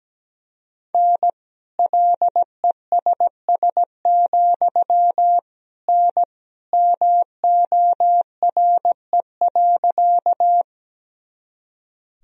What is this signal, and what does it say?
Morse code, heard as a keyed tone, about 17 wpm, 700 Hz. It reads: N LESS, N MORE.
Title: Sentences from Top 400 Words and Abbreviations 17wpm